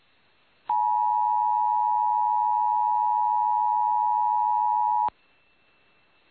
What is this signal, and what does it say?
Emergency Alert System (SAME) tones (the 853 and 960 Hz attention tone). It is heard from 0.7 to 5.1 s.
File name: eas / KUTTKGMT / KUTTKGMT